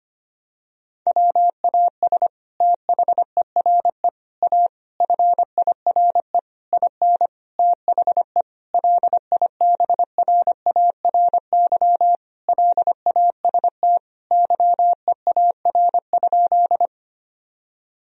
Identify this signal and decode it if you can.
Morse code, heard as a keyed tone, about 25 wpm, 700 Hz. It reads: WAS THERE A FIRE IN THE LIBRARY LAST YEAR?
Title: Sentences from Top 1000 Words - Review Part 4 - 25wpm